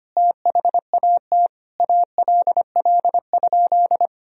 Morse code, heard as a keyed tone, about 25 wpm, 700 Hz. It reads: THAT ALL?
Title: Sentences from Top 200 Words and Abbreviations 25wpm